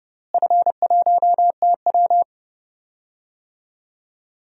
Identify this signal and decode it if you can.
Morse code, heard as a keyed tone, about 30 wpm, 700 Hz. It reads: F1TW
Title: DXCC-100 Call Signs - 4 Characters - Repeated 3x - 30wpm